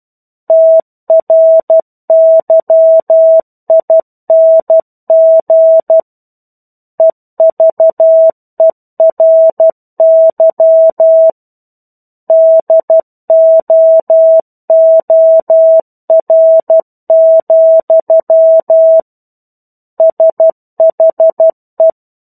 Morse code, heard as a keyed tone, about 12 wpm, 650 Hz. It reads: TRYING EVERY DOOR, SHE